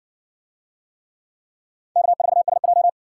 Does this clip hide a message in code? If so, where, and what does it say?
Yes: Morse code from 2.0 to 2.9 s, heard as a keyed tone, about 60 wpm, 700 Hz. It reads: K4SJ